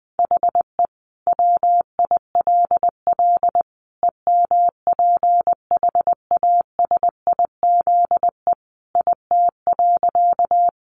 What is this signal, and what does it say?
Morse code, heard as a keyed tone, about 20 wpm, 700 Hz. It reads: HE WILL EMPHASIZE IT.
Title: Sentences from Top 1500 Words 20wpm